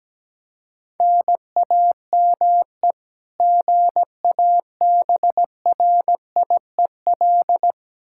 Morse code, heard as a keyed tone, about 17 wpm, 700 Hz. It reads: NAME GABRIEL
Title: K1USN SST QSO Practice 17wpm